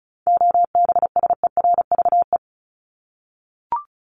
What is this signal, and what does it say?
Morse code, heard as a keyed tone, about 34 wpm, 700 Hz. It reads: OBSERVE
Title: ED Suffix - Top 100 Words with Base Words 35wpm